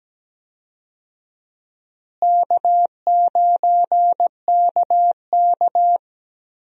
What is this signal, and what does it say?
Morse code, heard as a keyed tone, about 17 wpm, 700 Hz. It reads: K9KK
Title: USA Call Signs - 4 Characters - Repeated 3x - 17wpm